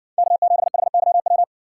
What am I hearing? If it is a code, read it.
Morse code, heard as a keyed tone, about 60 wpm, 700 Hz. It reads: K7HYW